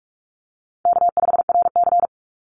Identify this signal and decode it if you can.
Morse code, heard as a keyed tone, about 44 wpm, 700 Hz. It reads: K5RC
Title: Call Signs Repeated 3x - 45wpm